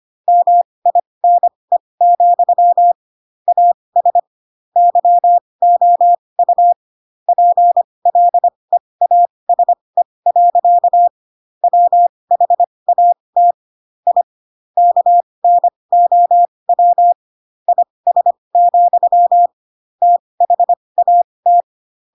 Morse code, heard as a keyed tone, about 25 wpm, spaced as for 20 wpm, 700 Hz. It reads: MINE, AS YOU PLEASE. WHAT I KNOW IS, THAT
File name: Pinocchio / Pinocchio ch4 15 25 ik1hge